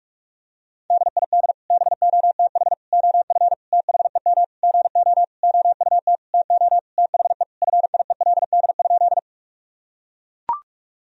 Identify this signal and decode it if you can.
Morse code, heard as a keyed tone, about 41 wpm, 700 Hz. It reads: DID BOTH OF THEM GO OUT TO THE FIELD?
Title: Sentences from Top 1000 Words - Review Part 2 - 45wpm